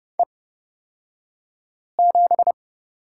Morse code, heard as a keyed tone, about 29 wpm, spaced as for 5 wpm, 700 Hz. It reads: E7
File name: Lesson 36d - BT - Two Characters 30wpm